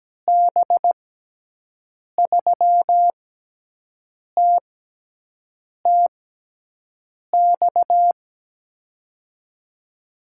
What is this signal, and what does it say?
Morse code, heard as a keyed tone, about 17 wpm, spaced as for 6 wpm, 700 Hz. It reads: B3TTX
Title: USA Call Signs - 6 Characters - 6x Character Spacing - 17wpm